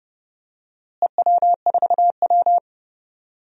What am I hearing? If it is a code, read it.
Morse code, heard as a keyed tone, about 29 wpm, 700 Hz. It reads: EW4W
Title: DXCC-100 Call Signs - 4 Characters - Repeated 3x - 30wpm